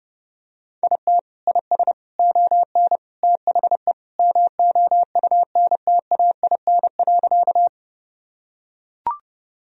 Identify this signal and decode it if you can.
Morse code, heard as a keyed tone, about 30 wpm, 700 Hz. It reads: IT IS ON THE MOUNTAIN.
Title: Sentences from Top 1000 Words - Review Part 1 - 30wpm